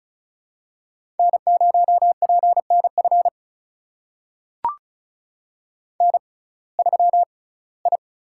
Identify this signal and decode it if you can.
Morse code, heard as a keyed tone, about 36 wpm, spaced as for 22 wpm, 700 Hz. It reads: N0PNF N3I
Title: USA Call Signs - 5 Characters - 6x Character Spacing - 35wpm